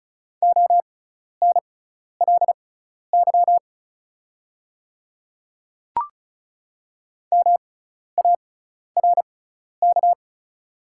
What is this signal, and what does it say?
Morse code, heard as a keyed tone, about 36 wpm, spaced as for 12 wpm, 700 Hz. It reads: ONLY MARK
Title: 6x Character Spacing - Top 300 Words - 35wpm